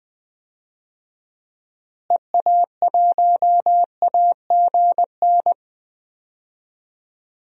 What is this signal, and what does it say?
Morse code, heard as a keyed tone, about 20 wpm, 700 Hz. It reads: EA1AGN
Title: DXCC-100 Call Signs - 6 Characters - Repeated 3x - 20wpm